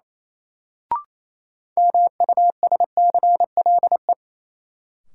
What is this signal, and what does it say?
Morse code, heard as a keyed tone, about 27 wpm, 700 Hz. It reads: MUSCLE
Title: Sentences from Top 1300 Words - New Word Drill - 28wpm